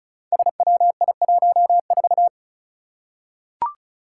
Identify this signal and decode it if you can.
Morse code, heard as a keyed tone, about 36 wpm, 700 Hz. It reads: SWI14
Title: Five Challenging Letter-Number Combinations 35wpm